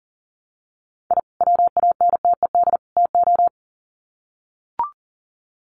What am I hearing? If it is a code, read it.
Morse code, heard as a keyed tone, about 40 wpm, 700 Hz. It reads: I WANTED TO
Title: Sets of 3 Words 40wpm